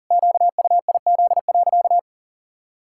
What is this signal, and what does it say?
Morse code, heard as a keyed tone, about 40 wpm, 700 Hz. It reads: QUIZ.
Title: Pangram Challenge v2 - 40wpm